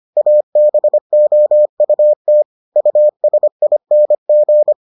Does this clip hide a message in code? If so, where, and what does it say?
Yes: Morse code from 0.2 to 4.7 s, heard as a keyed tone, about 25 wpm, 600 Hz. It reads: ABOUT USING